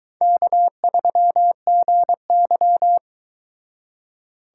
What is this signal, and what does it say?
Morse code, heard as a keyed tone, about 22 wpm, 700 Hz. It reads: K3GY